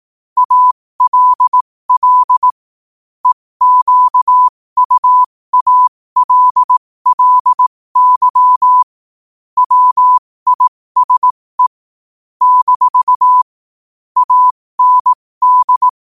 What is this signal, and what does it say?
Morse code, heard as a keyed tone, about 18 wpm, spaced as for 15 wpm, 1000 Hz. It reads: ALL EQUALLY WISE - AND
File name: qotdfiles / 15wpm